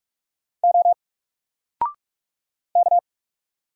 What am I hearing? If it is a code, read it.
Morse code, heard as a keyed tone, about 44 wpm, 700 Hz. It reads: O K
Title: Single Letter-Number 45wpm